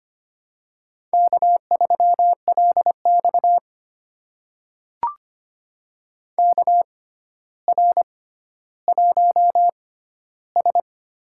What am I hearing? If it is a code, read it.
Morse code, heard as a keyed tone, about 25 wpm, spaced as for 13 wpm, 700 Hz. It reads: K3LX KR1S
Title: USA Call Signs - 4 Characters - 6x Character Spacing - 25wpm